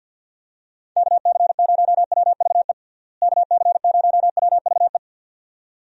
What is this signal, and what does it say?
Morse code, heard as a keyed tone, about 51 wpm, 700 Hz. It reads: KC0WVE KC0WVE
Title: USA Call Signs - 6 Characters - Repeated 2x - 50wpm